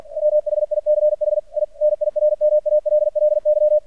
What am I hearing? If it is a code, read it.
Morse code, heard as a keyed tone, about 50 wpm, 600 Hz. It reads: MHICH I NIRMALLY
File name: archived / CRSnet